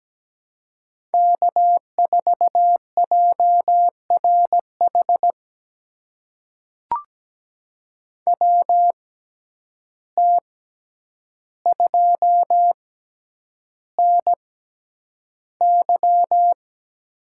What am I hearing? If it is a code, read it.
Morse code, heard as a keyed tone, about 17 wpm, spaced as for 9 wpm, 700 Hz. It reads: K4JRH WT2NY
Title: USA Call Signs - 5 Characters - 6x Character Spacing - 17wpm